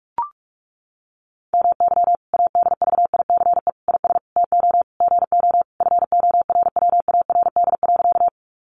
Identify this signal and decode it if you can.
Morse code, heard as a keyed tone, about 44 wpm, 700 Hz. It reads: MY ADVICE IS TO GO FORWARD.
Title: Sentences from Top 1300 Words 45wpm